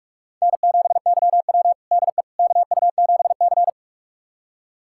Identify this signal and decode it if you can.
Morse code, heard as a keyed tone, about 46 wpm, 700 Hz. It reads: N7YW DE KU7C
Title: USA Callsign DE Callsign - 4 Characters - 45wpm